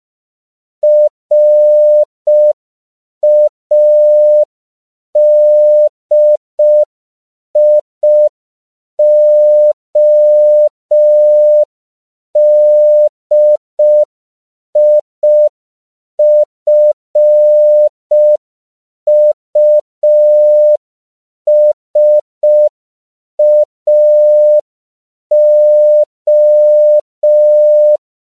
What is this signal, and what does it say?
Morse code, heard as a keyed tone, about 5 wpm, 600 Hz. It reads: RADIODIFUSAO
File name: CW / 5PPM